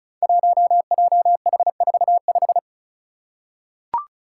Morse code, heard as a keyed tone, about 36 wpm, 700 Hz. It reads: 1JH45
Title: Five Challenging Letter-Number Combinations 35wpm